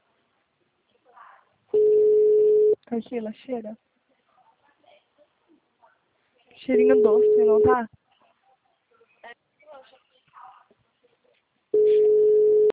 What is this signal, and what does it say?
Ringback tone (Europe, 425 Hz), with speech.
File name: out- -2017- - - .61977